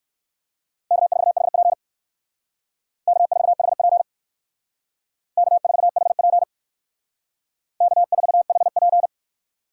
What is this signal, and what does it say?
Morse code, heard as a keyed tone, about 57 wpm, 700 Hz. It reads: K4HP K4HP K4HP K4HP